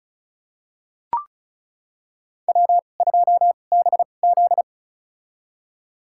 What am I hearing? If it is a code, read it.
Morse code, heard as a keyed tone, about 34 wpm, spaced as for 25 wpm, 700 Hz. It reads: W2BZ
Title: USA Call Signs - 4 Characters - 2x Character Spacing - 35wpm